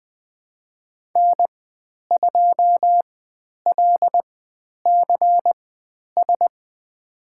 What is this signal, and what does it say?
Morse code, heard as a keyed tone, about 20 wpm, spaced as for 10 wpm, 700 Hz. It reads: N2LCS
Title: 095 Lesson 16g - Letter c - Call Signs 20wpm